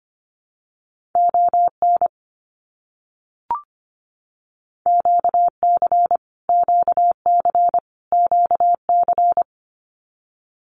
Morse code, heard as a keyed tone, about 24 wpm, 700 Hz. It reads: ON QC QC QC
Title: Canadian Provinces and Territories - Repeated 3x - 25wpm